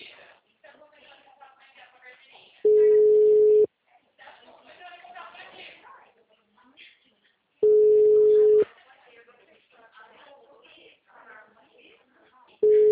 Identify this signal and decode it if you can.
Ringback tone (Europe, 425 Hz), with speech.
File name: out- -2002- - - .29661